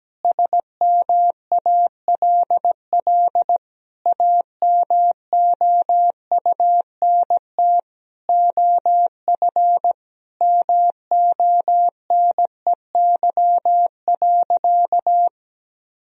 Morse code, heard as a keyed tone, about 17 wpm, 700 Hz. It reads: SMALL AMOUNT OF MONEY.
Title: Sentences from Top 1000 Words - Review Part 4 - 17wpm